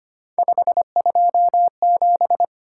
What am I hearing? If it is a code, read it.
Morse code, heard as a keyed tone, about 29 wpm, 700 Hz. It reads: 527
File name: Speed-Racing - Three Letter-Number Combinations - 25wpm